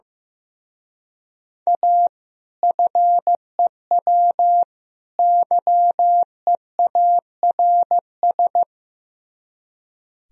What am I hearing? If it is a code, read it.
Morse code, heard as a keyed tone, about 15 wpm, 700 Hz. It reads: A FEW YEARS
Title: Speed-Racing - Sets of 3 Words 15wpm